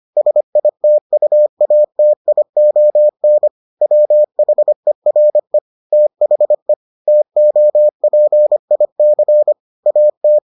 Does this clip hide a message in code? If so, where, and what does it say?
Yes: Morse code from 0.2 to 10.4 s, heard as a keyed tone, about 25 wpm, 600 Hz. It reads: SITUATION WHERE THE TOPIC AT